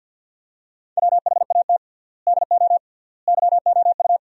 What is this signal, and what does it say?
Morse code, heard as a keyed tone, about 50 wpm, 700 Hz. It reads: WHAT DO YOU
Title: 3x Word-Spacing - Sentences from Top 200 Words - 50wpm